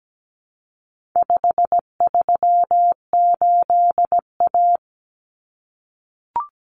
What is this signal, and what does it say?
Morse code, heard as a keyed tone, about 17 wpm, 700 Hz. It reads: 538A
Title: Common Cut Numbers - 4-Digit 17wpm